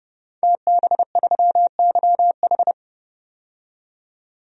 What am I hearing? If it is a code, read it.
Morse code, heard as a keyed tone, about 30 wpm, 700 Hz. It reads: TB3YH
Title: DXCC-100 Call Signs - 5 Characters - 30wpm